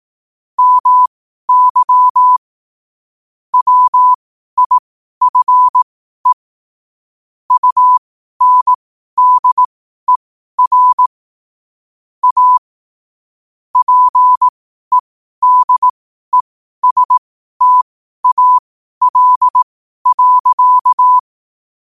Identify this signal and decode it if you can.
Morse code, heard as a keyed tone, about 18 wpm, spaced as for 12 wpm, 1000 Hz. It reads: MY WIFE UNDER A PEDESTAL.